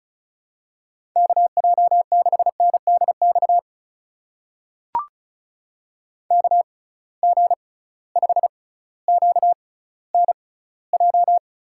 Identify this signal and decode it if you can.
Morse code, heard as a keyed tone, about 34 wpm, spaced as for 18 wpm, 700 Hz. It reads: KJ6NDX KG5QNJ
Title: USA Call Signs - 6 Characters - 6x Character Spacing - 35wpm